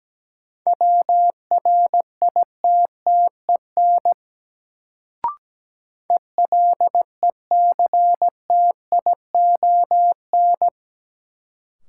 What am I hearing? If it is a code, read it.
Morse code, heard as a keyed tone, about 17 wpm, 700 Hz. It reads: WRITTEN ELECTION
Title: Sentences from Top 1500 Words - New Word Drill Alt - 17wpm